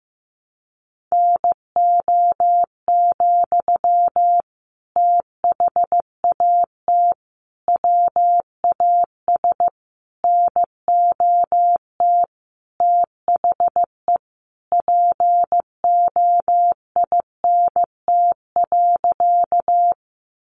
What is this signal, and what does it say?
Morse code, heard as a keyed tone, about 15 wpm, 700 Hz. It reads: NO, THAT WAS NOT THE POINT.
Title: Sentences from Top 600 Words 15wpm